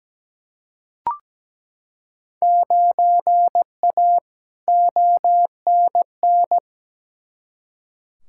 Morse code, heard as a keyed tone, about 17 wpm, 700 Hz. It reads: 9A ONN
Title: Field Day Exchanges Encore - 17wpm